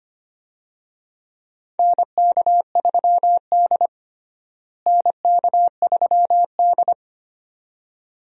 Morse code, heard as a keyed tone, about 25 wpm, 700 Hz. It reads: NK3D NK3D
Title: USA Call Signs - 4 Characters - Repeated 2x - 25wpm